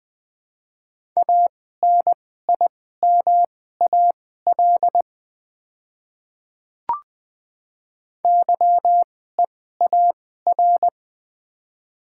Morse code, heard as a keyed tone, about 20 wpm, spaced as for 15 wpm, 700 Hz. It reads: ANIMAL YEAR
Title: 2x Character Spacing - Top 200 Words - 20wpm